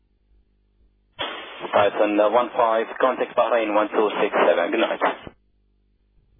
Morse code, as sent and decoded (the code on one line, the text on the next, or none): none